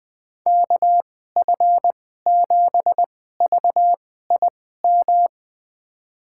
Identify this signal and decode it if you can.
Morse code, heard as a keyed tone, about 21 wpm, spaced as for 15 wpm, 700 Hz. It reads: KF7VIM
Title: USA Call Signs - 6 Characters - 2x Character Spacing - 20wpm